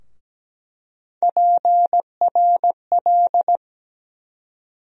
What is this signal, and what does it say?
Morse code, heard as a keyed tone, about 17 wpm, 700 Hz. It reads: PRL
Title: USA Call Sign Postfixes - 3 Letters 17wpm